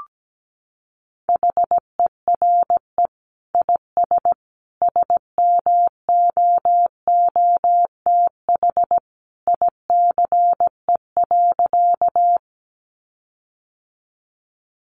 Morse code, heard as a keyed tone, about 17 wpm, 700 Hz. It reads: HERE IS SMOOTH ICE.
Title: Sentences from Top 1400 Words 17wpm